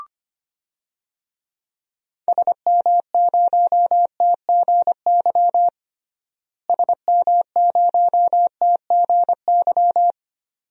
Morse code, heard as a keyed tone, about 25 wpm, 700 Hz. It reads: SM0TGY SM0TGY
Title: DXCC-100 Call Signs - 6 Characters - Repeated 2x - 25wpm